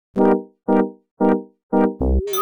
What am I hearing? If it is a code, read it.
There is no code.